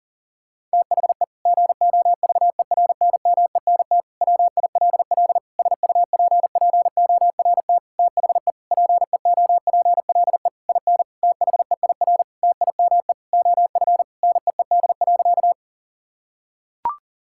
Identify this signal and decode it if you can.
Morse code, heard as a keyed tone, about 40 wpm, 700 Hz. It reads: THE GOVERNMENT WILL SUPPORT THE PEOPLE IN THEIR TIME OF NEED.
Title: Sentences from Top 1000 Words - Review Part 5 - 40wpm